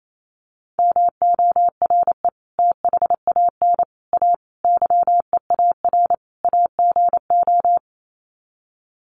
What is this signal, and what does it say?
Morse code, heard as a keyed tone, about 28 wpm, 700 Hz. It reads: MORE THAN A YEAR AGO
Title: Sets of 5 Words Encore2 28wpm